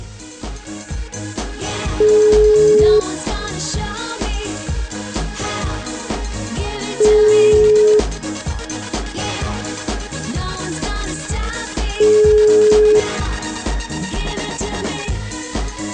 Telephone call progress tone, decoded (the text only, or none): ringback tone